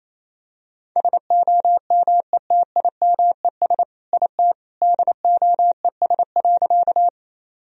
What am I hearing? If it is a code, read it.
Morse code, heard as a keyed tone, about 28 wpm, 700 Hz. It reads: SOMETIMES IT DOES.
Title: Sentences from Top 1000 Words - Review Part 3 - 28wpm